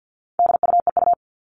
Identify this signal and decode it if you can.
Morse code, heard as a keyed tone, about 50 wpm, 700 Hz. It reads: DUEU
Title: All Cut Numbers - 4-Digit 50wpm